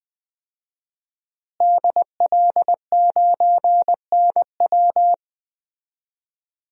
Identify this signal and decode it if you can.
Morse code, heard as a keyed tone, about 20 wpm, 700 Hz. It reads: DL9NW
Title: DXCC-100 Call Signs - 5 Characters - Repeated 2x - 20wpm